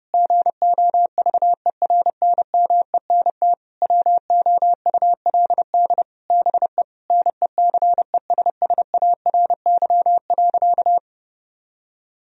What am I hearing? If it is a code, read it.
Morse code, heard as a keyed tone, about 30 wpm, 700 Hz. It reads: GOVERNMENT WOULD BE NECESSARY.
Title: Sentences from Top 1000 Words - Review Part 3 - 30wpm